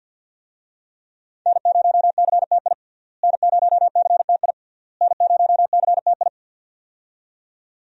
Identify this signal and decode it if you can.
Morse code, heard as a keyed tone, about 50 wpm, 700 Hz. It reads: N0CTI N0CTI N0CTI
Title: USA Call Signs - 5 Characters - Repeated 3x - 50wpm